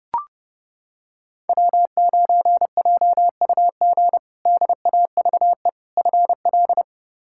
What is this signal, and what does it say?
Morse code, heard as a keyed tone, about 30 wpm, 700 Hz. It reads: W9JUG DAVE FL
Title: K1USN SST QSO Practice - 30wpm